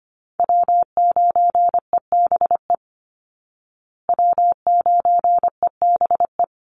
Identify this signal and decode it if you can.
Morse code, heard as a keyed tone, about 25 wpm, 700 Hz. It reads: W9EBE W9EBE
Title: Call Signs Repeated 3x - 25wpm